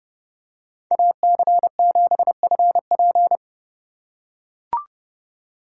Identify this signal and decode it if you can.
Morse code, heard as a keyed tone, about 32 wpm, 700 Hz. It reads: AC7FP
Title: USA Call Signs - 5 Characters - 4x Character Spacing - 30wpm